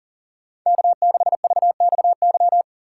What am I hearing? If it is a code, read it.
Morse code, heard as a keyed tone, about 40 wpm, 700 Hz. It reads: K6VXY
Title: USA Call Signs - 5 Characters - 4x Character Spacing - 40wpm